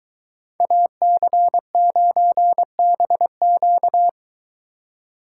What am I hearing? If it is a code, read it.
Morse code, heard as a keyed tone, about 22 wpm, 700 Hz. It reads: AC9BQ